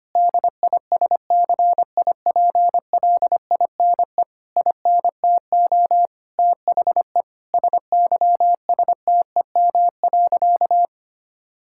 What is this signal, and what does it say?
Morse code, heard as a keyed tone, about 25 wpm, 700 Hz. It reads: DISCIPLINE INTO THE SYSTEM.
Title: Sentences from Top 1000 Words - Review Part 6 - 25wpm